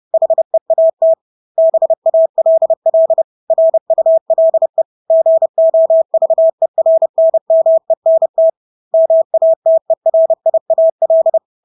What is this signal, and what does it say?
Morse code, heard as a keyed tone, about 30 wpm, 650 Hz. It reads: HEAT BALL RULE GOVERNMENT MATERIAL
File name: Engelske ord 30-30 wpm